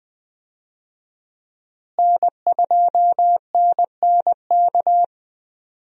Morse code, heard as a keyed tone, about 20 wpm, 700 Hz. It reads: N2NNK
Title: USA Call Signs - 5 Characters - Repeated 2x - 20wpm